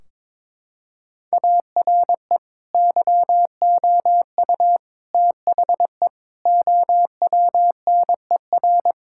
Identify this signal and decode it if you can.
Morse code, heard as a keyed tone, about 22 wpm, 700 Hz. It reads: ARE YOU THE OWNER
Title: Sentences from Top 1600 Words 22wpm